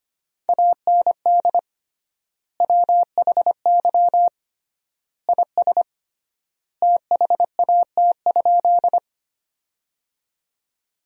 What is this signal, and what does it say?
Morse code, heard as a keyed tone, about 25 wpm, 700 Hz. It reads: AND WHY IS THAT?
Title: 3x Word-Spacing - Sentences from Top 200 Words - 25wpm